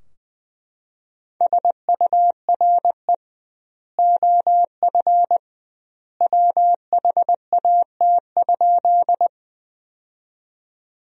Morse code, heard as a keyed tone, about 20 wpm, 700 Hz. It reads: SURE OF WHAT?
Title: 2x Word-Spacing - Sentences from Top 300 Words - 20wpm